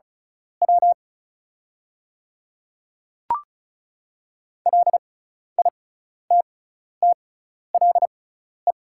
Morse code, heard as a keyed tone, about 34 wpm, spaced as for 12 wpm, 700 Hz. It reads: W LITTLE
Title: 6x Character Spacing - Top 200 Words - 35wpm